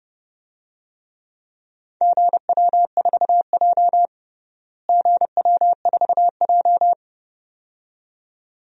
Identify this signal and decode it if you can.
Morse code, heard as a keyed tone, about 30 wpm, 700 Hz. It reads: GW4J GW4J
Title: DXCC-100 Call Signs - 4 Characters - Repeated 2x - 30wpm